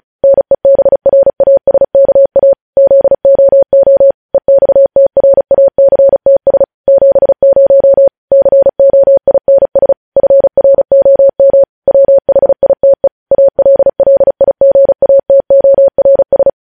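Morse code, heard as a keyed tone, about 36 wpm, 550 Hz. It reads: NEBRASKA ZOO EXTRACTS 70 COINS FROM WHITE ALLIGATORS
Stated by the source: Morse code news headlines sent at 35.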